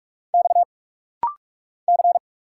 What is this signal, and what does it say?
Morse code, heard as a keyed tone, about 46 wpm, 690 Hz. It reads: X C
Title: Single Letters - Rapid-Fire 45wpm